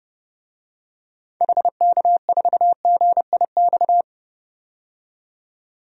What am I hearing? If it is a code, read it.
Morse code, heard as a keyed tone, about 30 wpm, 700 Hz. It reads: HK4GIX
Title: DXCC-100 Call Signs - 6 Characters - 30wpm